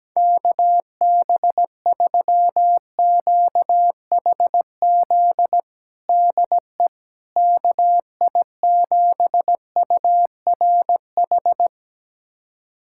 Morse code, heard as a keyed tone, about 17 wpm, 700 Hz. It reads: KB3QHZ DE KI7URH
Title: USA Callsign DE Callsign - 6 Characters - 17wpm